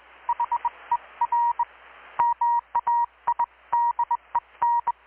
Morse code, heard as a keyed tone, about 21 wpm, 950 Hz, with radio noise. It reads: HER MAIDEN
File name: blog / swl / 6-abril-2019-12695-USB-KPS